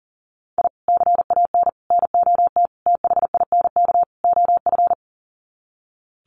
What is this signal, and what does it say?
Morse code, heard as a keyed tone, about 40 wpm, 700 Hz. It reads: I CAN NOT THINK OF
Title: Sets of 5 Words 40wpm